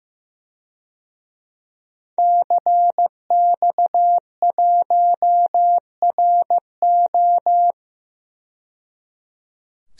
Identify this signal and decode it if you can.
Morse code, heard as a keyed tone, about 15 wpm, 700 Hz. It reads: CX1RO
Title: DXCC-100 Call Signs - 5 Characters - 15wpm